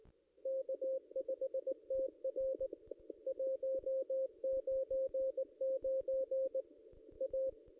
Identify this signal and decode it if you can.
Morse code, heard as a keyed tone, about 19 wpm, 530 Hz. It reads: K5TR 199 A
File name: contests / ss-cw-k5tr-op-n5rz -1600